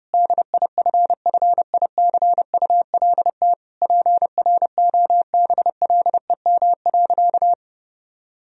Morse code, heard as a keyed tone, about 30 wpm, 700 Hz. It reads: DIFFICULT PROBLEM.
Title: Sentences from Top 1000 Words - Review Part 2 - 30wpm